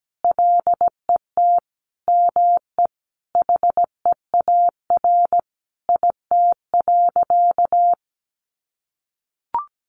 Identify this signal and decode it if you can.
Morse code, heard as a keyed tone, about 17 wpm, 700 Hz. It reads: LET ME HEAR IT.
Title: Sentences from Top 1500 Words 17wpm